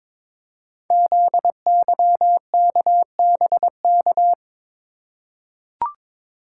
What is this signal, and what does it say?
Morse code, heard as a keyed tone, about 23 wpm, 700 Hz. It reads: ZYKBK
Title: Code Groups - 5 Letters - 22wpm